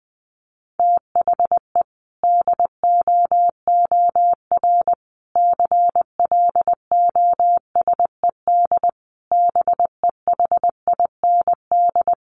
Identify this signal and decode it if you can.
Morse code, heard as a keyed tone, about 19 wpm, 700 Hz. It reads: THE DOOR CLOSED BEHIND
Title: Sentences from Top 1400 Words 20wpm